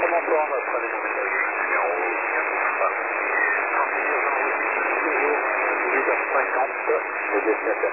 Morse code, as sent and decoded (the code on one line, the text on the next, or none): none